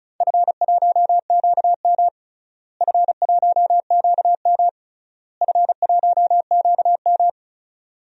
Morse code, heard as a keyed tone, about 34 wpm, 700 Hz. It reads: F1QM F1QM F1QM
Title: DXCC-100 Call Signs - 4 Characters - Repeated 3x - 35wpm